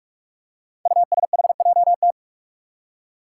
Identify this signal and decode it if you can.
Morse code, heard as a keyed tone, about 46 wpm, 700 Hz. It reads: USHJT
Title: Five Challenging Letter-Number Combinations 45wpm